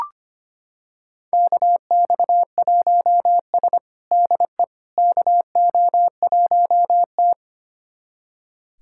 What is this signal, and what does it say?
Morse code, heard as a keyed tone, about 25 wpm, 700 Hz. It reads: KX1S DE KO1T
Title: USA Callsign DE Callsign - 4 Characters - 25wpm